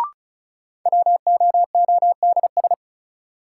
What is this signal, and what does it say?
Morse code, heard as a keyed tone, about 36 wpm, 700 Hz. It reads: WOODS
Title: Sentences from Top 1000 Words - Review Part 2 - New Word Drill - 35wpm